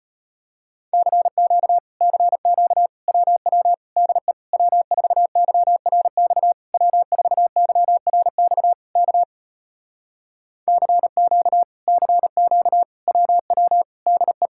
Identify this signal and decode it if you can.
Morse code, heard as a keyed tone, about 33 wpm, 700 Hz. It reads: CQ CQ WW DE W4YRX W4YRX K CQ CQ WW DE